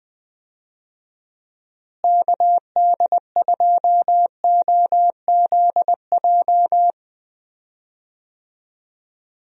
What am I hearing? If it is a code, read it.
Morse code, heard as a keyed tone, about 20 wpm, 700 Hz. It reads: KD2OZJ
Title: USA Call Signs - 6 Characters - Repeated 2x - 20wpm